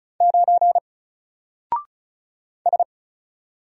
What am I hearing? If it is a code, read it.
Morse code, heard as a keyed tone, about 34 wpm, 700 Hz. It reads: 9 S
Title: Single Character (Letter:Number:Punctuation:Prosign) 35wpm